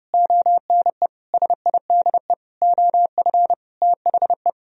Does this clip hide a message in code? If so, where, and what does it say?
Yes: Morse code from 0.1 to 4.5 s, heard as a keyed tone, about 39 wpm, 700 Hz. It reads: ONE SIDE OF THE